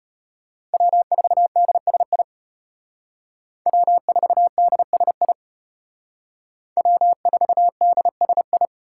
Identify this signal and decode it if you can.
Morse code, heard as a keyed tone, about 33 wpm, 700 Hz. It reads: W4DSI W4DSI W4DSI